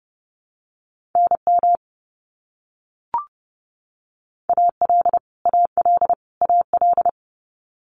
Morse code, heard as a keyed tone, about 30 wpm, 700 Hz. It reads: NM AL AL AL
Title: US States Repeated 3x 30wpm